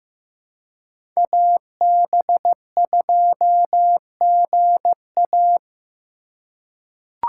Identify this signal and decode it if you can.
Morse code, heard as a keyed tone, about 15 wpm, 700 Hz. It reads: AB2GA
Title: USA Call Signs - 5 Characters - 4x Character Spacing - 15wpm